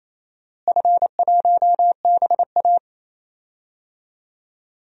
Morse code, heard as a keyed tone, about 28 wpm, 700 Hz. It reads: F1BA
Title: DXCC-100 Call Signs - 4 Characters - Repeated 3x - 28wpm